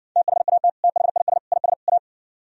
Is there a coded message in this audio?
Yes: Morse code from 0.2 to 2.0 s, heard as a keyed tone, about 60 wpm, 700 Hz. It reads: THAT THIS IS A